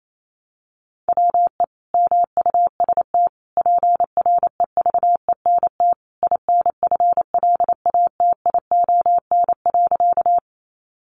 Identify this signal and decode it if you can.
Morse code, heard as a keyed tone, about 28 wpm, 700 Hz. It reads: WE MUST PREVENT INFLATION.
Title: Sentences from Top 1100 Words 28wpm